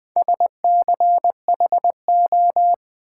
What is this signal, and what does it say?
Morse code, heard as a keyed tone, about 20 wpm, 700 Hz. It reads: SCHO
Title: Build A Word - 4 Letters At A Time - 11-Letter Words - 20wpm